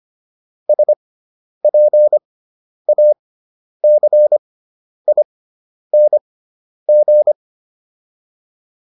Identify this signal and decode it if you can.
Morse code, heard as a keyed tone, about 25 wpm, spaced as for 10 wpm, 600 Hz. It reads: SPACING